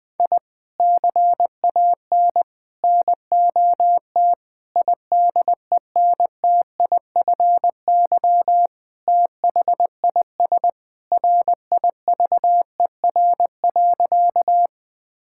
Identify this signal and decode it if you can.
Morse code, heard as a keyed tone, about 21 wpm, 700 Hz. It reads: I CAN NOT IDENTIFY THIS RIVER.
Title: Sentences from Top 1000 Words - Review Part 6 - 20wpm